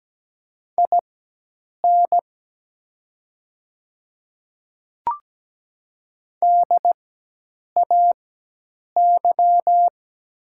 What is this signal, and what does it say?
Morse code, heard as a keyed tone, about 17 wpm, spaced as for 8 wpm, 700 Hz. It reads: IN DAY
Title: 4x Character Spacing - Top 100 Words - 17wpm